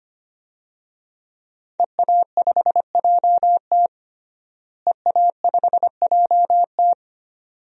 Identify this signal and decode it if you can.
Morse code, heard as a keyed tone, about 25 wpm, 700 Hz. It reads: EA5JT EA5JT
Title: DXCC-100 Call Signs - 5 Characters - Repeated 3x - 25wpm